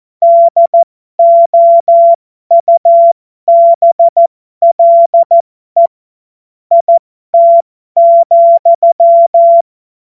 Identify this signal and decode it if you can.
Morse code, heard as a keyed tone, about 14 wpm, spaced as for 12 wpm, 680 Hz. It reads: DOUBLE IT,
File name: Morsekurs / Witze